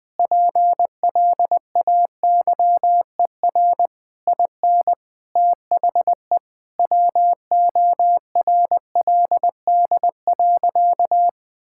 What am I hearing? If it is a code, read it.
Morse code, heard as a keyed tone, about 21 wpm, 700 Hz. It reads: PLAYER IN THE WORLD.
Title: Sentences with 6-Word Phrases - 20wpm